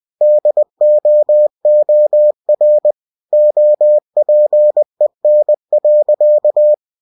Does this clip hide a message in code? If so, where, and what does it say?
Yes: Morse code from 0.2 to 6.7 s, heard as a keyed tone, about 20 wpm, 600 Hz. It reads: DOOR OPEN.